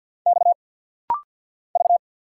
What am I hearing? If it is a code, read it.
Morse code, heard as a keyed tone, about 50 wpm, 700 Hz. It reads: X V
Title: Single Letter-Number Alt - Rapid-Fire 50wpm